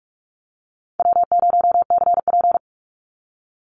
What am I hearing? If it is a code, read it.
Morse code, heard as a keyed tone, about 44 wpm, 700 Hz. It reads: W0CP
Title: Easy Call Signs - 45wpm